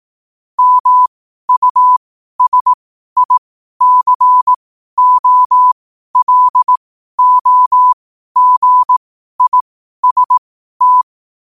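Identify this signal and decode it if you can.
Morse code, heard as a keyed tone, about 18 wpm, spaced as for 12 wpm, 1000 Hz. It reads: MUSICOLOGIST
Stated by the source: 12 wpm Morse.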